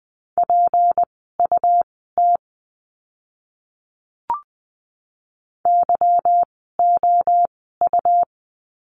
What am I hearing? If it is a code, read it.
Morse code, heard as a keyed tone, about 20 wpm, spaced as for 14 wpm, 700 Hz. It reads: PUT YOU
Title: 192 Lesson 31f - Letter v - Words 20wpm